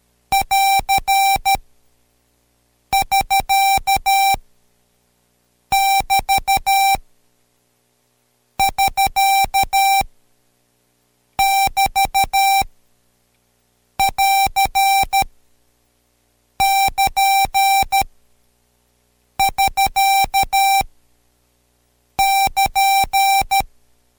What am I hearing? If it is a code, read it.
Morse code, heard as a keyed tone, about 13 wpm, 780 Hz. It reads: <AR> <SK> <BT> <SK> <BT> <AR> <KN> <SK> <KN>
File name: Code Quick / AudioDownld2 / Seg127-Code Quick Prosigns 2